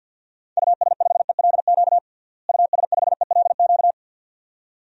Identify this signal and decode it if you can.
Morse code, heard as a keyed tone, about 46 wpm, 700 Hz. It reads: US5ELQ US5ELQ
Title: DXCC-100 Call Signs - 6 Characters - Repeated 2x - 50wpm